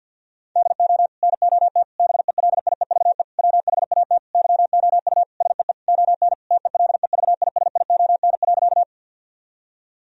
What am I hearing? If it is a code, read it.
Morse code, heard as a keyed tone, about 50 wpm, 700 Hz. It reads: DO NOT BELIEVE WHAT YOU SEE ON TELEVISION.
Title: Sentences from Top 1000 Words - Review Part 4 - 50wpm